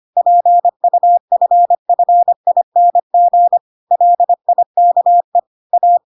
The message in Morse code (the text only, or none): PUFFING LIKE A